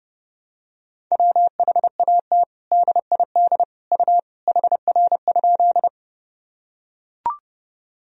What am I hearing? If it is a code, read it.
Morse code, heard as a keyed tone, about 28 wpm, 700 Hz. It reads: WHAT DID U HR?
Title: Sentences from Top 300 Words and Abbreviations 30wpm